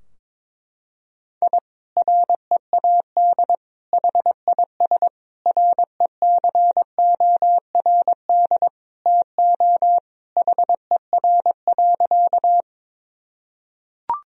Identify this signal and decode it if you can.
Morse code, heard as a keyed tone, about 22 wpm, 700 Hz. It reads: I READ HIS RECORD TO HER.
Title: Sentences from Top 1000 Words - Review Part 3 - 22wpm